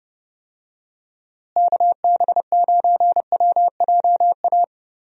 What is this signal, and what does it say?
Morse code, heard as a keyed tone, about 30 wpm, 700 Hz. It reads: KB9WJA
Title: USA Call Signs - 6 Characters - Repeated 3x - 30wpm